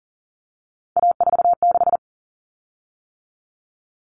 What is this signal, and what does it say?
Morse code, heard as a keyed tone, about 40 wpm, 700 Hz. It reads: A46
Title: Common Cut Numbers - 3-Digit 40wpm